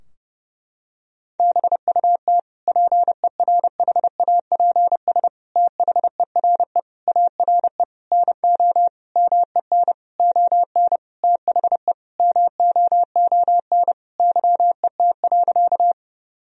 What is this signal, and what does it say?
Morse code, heard as a keyed tone, about 30 wpm, 700 Hz. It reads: BUT PERHAPS THERE ARE NO MEN ON THE MOON YET.
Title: Sentences from Top 1000 Words - Review Part 3 - 30wpm